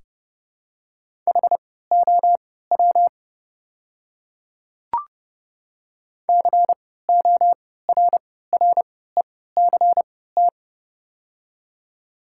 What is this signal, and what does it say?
Morse code, heard as a keyed tone, about 30 wpm, spaced as for 17 wpm, 690 Hz. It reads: HOW CORRECT
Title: 3x Character Spacing - Top 500 Words - 30wpm